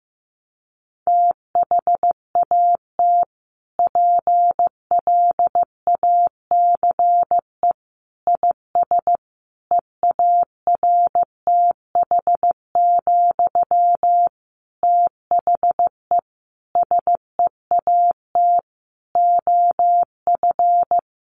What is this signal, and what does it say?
Morse code, heard as a keyed tone, about 15 wpm, 700 Hz. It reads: THAT PLACE IS EARTH, THE SEAT OF
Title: Sentences from Top 1500 Words 15wpm